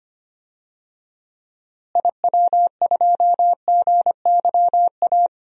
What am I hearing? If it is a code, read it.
Morse code, heard as a keyed tone, about 25 wpm, 700 Hz. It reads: IW2GYA
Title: DXCC-100 Call Signs - 6 Characters - Repeated 2x - 25wpm